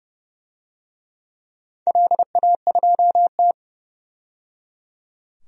Morse code, heard as a keyed tone, about 30 wpm, 700 Hz. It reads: LA2T
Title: DXCC-100 Call Signs - 4 Characters - 30wpm